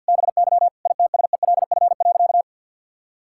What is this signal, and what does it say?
Morse code, heard as a keyed tone, about 50 wpm, 700 Hz. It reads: BY ITSELF.
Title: Sentences from Top 1000 Words - Review Part 5 - 50wpm